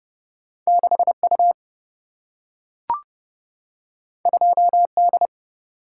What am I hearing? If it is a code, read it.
Morse code, heard as a keyed tone, about 30 wpm, 700 Hz. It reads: 6U 2D
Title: Single-Letter Single-Number 30wpm